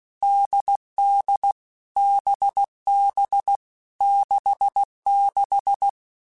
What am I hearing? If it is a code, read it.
Morse code, heard as a keyed tone, about 16 wpm, 800 Hz. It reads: DD BB 66
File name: Short Runs / Short Runs-mp3s / NDB NDB6 @15